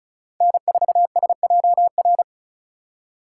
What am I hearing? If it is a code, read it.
Morse code, heard as a keyed tone, about 46 wpm, 700 Hz. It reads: N4SJR